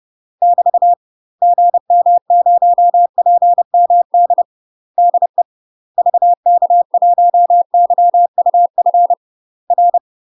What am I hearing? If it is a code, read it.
Morse code, heard as a keyed tone, about 30 wpm, 700 Hz. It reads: <BT> GM0PMD DE VK1YUF R
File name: UK MASTER QSOs TRAINING / 30WPM / UK MASTER QSOs 0041